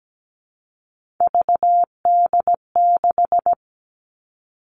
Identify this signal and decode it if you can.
Morse code, heard as a keyed tone, about 17 wpm, 700 Hz. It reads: VD6
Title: All Cut Numbers - 3-Digit 17wpm